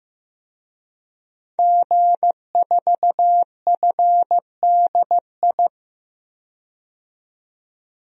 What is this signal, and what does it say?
Morse code, heard as a keyed tone, about 16 wpm, 700 Hz. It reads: G4FDI
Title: DXCC-100 Call Signs - 5 Characters - 15wpm